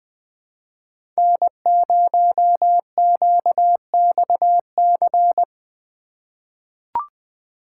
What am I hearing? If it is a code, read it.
Morse code, heard as a keyed tone, about 20 wpm, 700 Hz. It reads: N0QXC
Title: USA Call Signs - 5 Characters - 3x Character Spacing - 20wpm